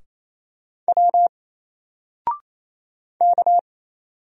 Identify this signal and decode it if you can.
Morse code, heard as a keyed tone, about 28 wpm, 700 Hz. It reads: W K
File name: Single-Letter-Number-28wpm